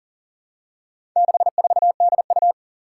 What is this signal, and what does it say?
Morse code, heard as a keyed tone, about 39 wpm, 700 Hz. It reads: 64DU
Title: Four Challenging Letter-Number Combinations 40wpm